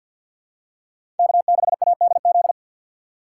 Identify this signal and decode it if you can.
Morse code, heard as a keyed tone, about 52 wpm, 700 Hz. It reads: K6ADZ
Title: USA Call Signs - 5 Characters - 6x Character Spacing - 50wpm